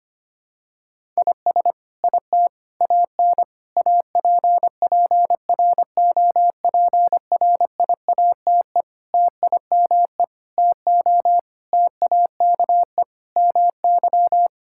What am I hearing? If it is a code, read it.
Morse code, heard as a keyed tone, about 25 wpm, 700 Hz. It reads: IS IT AN APPROPRIATE TIME TO TAKE MY
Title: Sentences from Top 1000 Words - Review Part 5 - 25wpm